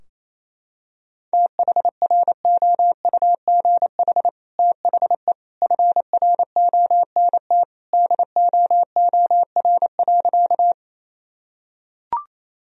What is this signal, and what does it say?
Morse code, heard as a keyed tone, about 27 wpm, 700 Hz. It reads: THROUGH THE FRONT DOOR.
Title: Sentences from Top 1000 Words - Review Part 2 - 28wpm